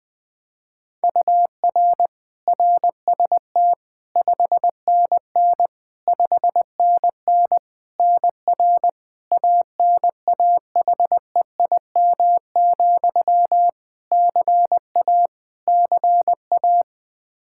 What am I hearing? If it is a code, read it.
Morse code, heard as a keyed tone, about 21 wpm, 700 Hz. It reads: UR RST 5NN 5NN NR ANAHEIM, CA CA
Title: Practice QSO - Exchange 1 - Beginning - 20wpm